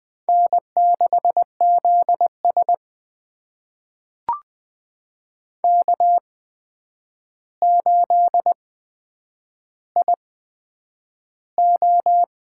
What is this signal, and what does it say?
Morse code, heard as a keyed tone, about 20 wpm, spaced as for 9 wpm, 700 Hz. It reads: N6ZS K8IO
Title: USA Call Signs - 4 Characters - 8x Character Spacing - 20wpm